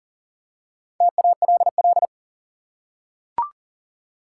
Morse code, heard as a keyed tone, about 40 wpm, 700 Hz. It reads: TALL
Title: Four Letter Words - 40wpm